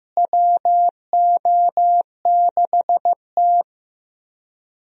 Morse code, heard as a keyed tone, about 15 wpm, 700 Hz. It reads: WO6T
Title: USA Call Signs - 4 Characters - Repeated 3x - 15wpm